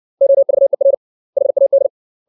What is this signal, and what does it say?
Morse code, heard as a keyed tone, about 61 wpm, 550 Hz. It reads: OVER HAD
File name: 632 words / 632 Words 60WPM